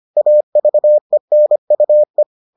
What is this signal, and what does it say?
Morse code, heard as a keyed tone, about 25 wpm, 600 Hz. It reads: AVENUE